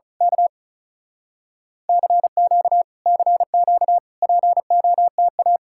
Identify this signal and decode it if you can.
Morse code, heard as a keyed tone, about 32 wpm, 700 Hz. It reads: K CQ CQ POTA
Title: Speed-Racing Common QSO Phrases - 35wpm